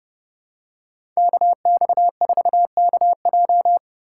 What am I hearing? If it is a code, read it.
Morse code, heard as a keyed tone, about 32 wpm, 700 Hz. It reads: KX4KJ
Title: USA Call Signs - 5 Characters - 2x Character Spacing - 30wpm